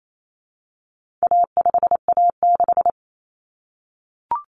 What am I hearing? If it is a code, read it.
Morse code, heard as a keyed tone, about 28 wpm, 700 Hz. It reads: A5A6
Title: Common Cut Numbers - 4-Digit 28wpm